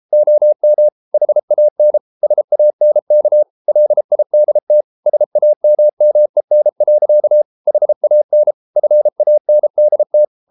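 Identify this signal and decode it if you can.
Morse code, heard as a keyed tone, about 33 wpm, 600 Hz. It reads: OM HAN SANK LIDT SAMMEN. HAN FANDT
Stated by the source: Morse code at 33 wpm, QRQ.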